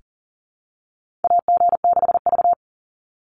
Morse code, heard as a keyed tone, about 41 wpm, 700 Hz. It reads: AG6V
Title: Call Signs - 40wpm